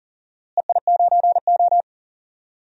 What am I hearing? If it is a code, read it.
Morse code, heard as a keyed tone, about 40 wpm, 700 Hz. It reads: EI9O
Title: DXCC-100 Call Signs - 4 Characters - Repeated 3x - 40wpm